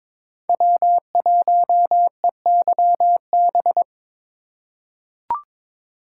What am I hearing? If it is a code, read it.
Morse code, heard as a keyed tone, about 22 wpm, 700 Hz. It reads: W1EYB
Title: USA Call Signs - 5 Characters - 4x Character Spacing - 22wpm